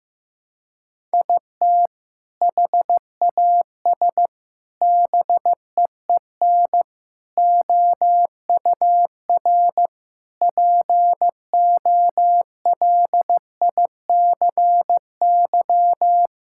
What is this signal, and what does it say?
Morse code, heard as a keyed tone, about 15 wpm, 700 Hz. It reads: IT HAS BEEN OUR POLICY
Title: Sentences from Top 1000 Words - Review Part 5 - 15wpm